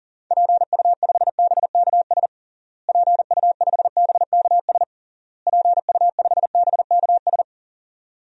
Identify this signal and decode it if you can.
Morse code, heard as a keyed tone, about 40 wpm, 700 Hz. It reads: PU5BKS PU5BKS PU5BKS
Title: DXCC-100 Call Signs - 6 Characters - Repeated 3x - 40wpm